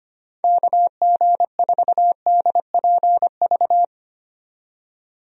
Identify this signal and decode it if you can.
Morse code, heard as a keyed tone, about 25 wpm, 700 Hz. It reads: KG4DPV
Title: USA Call Signs - 6 Characters - 25wpm